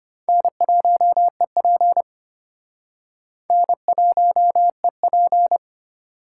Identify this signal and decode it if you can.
Morse code, heard as a keyed tone, about 31 wpm, 700 Hz. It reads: N1EP N1EP